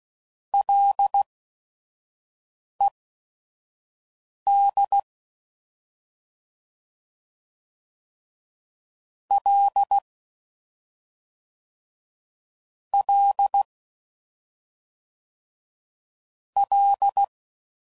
Morse code, heard as a keyed tone, about 16 wpm, spaced as for 5 wpm, 800 Hz. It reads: LED L L L
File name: Prefixes / FLs & THs / Single Fs Ls